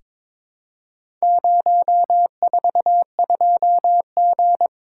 Morse code, heard as a keyed tone, about 22 wpm, 700 Hz. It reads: 042G
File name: practice / 22wpm / Four Letter-Number Combinations - 22wpm